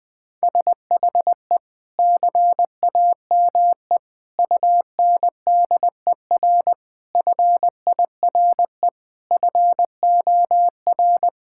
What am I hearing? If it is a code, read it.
Morse code, heard as a keyed tone, about 20 wpm, 700 Hz. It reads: SHE CAME UNDER FIRE FOR
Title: Sentences from Top 1000 Words - Review Part 6 - 20wpm